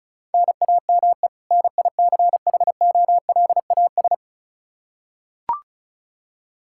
Morse code, heard as a keyed tone, about 34 wpm, 700 Hz. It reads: NAME NICHOLAS
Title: K1USN SST QSO Practice 35wpm